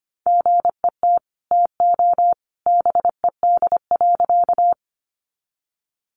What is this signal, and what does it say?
Morse code, heard as a keyed tone, about 25 wpm, 700 Hz. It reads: GET TO BED.
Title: Sentences from Top 500 Words 25wpm